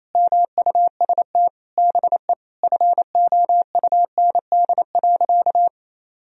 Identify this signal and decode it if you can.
Morse code, heard as a keyed tone, about 28 wpm, 700 Hz. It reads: MUST BE FOUND.
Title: Sentences from Top 1000 Words - Review Part 4 - 28wpm